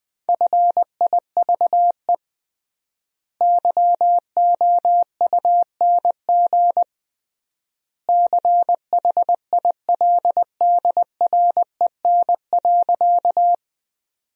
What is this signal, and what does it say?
Morse code, heard as a keyed tone, about 19 wpm, 700 Hz. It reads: FIVE YOUNG CHILDREN.
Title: 3x Word-Spacing - Sentences from Top 400 Words - 20wpm